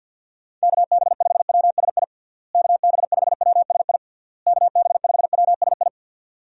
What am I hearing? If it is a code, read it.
Morse code, heard as a keyed tone, about 52 wpm, 700 Hz. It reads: KB5WSI KB5WSI KB5WSI
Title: USA Call Signs - 6 Characters - Repeated 3x - 50wpm